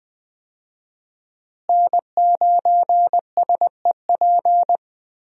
Morse code, heard as a keyed tone, about 20 wpm, 700 Hz. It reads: N9SEP
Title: USA Call Signs - 5 Characters - Repeated 2x - 20wpm